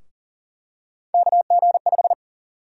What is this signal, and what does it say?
Morse code, heard as a keyed tone, about 38 wpm, 700 Hz. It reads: KG5
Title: USA Call Sign Prefixes - 40wpm